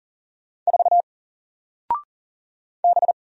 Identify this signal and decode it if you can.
Morse code, heard as a keyed tone, about 40 wpm, 700 Hz. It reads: 4 B
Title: Single Letter-Number Alt 40wpm